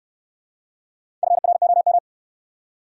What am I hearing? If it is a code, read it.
Morse code, heard as a keyed tone, about 60 wpm, 700 Hz. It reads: FROM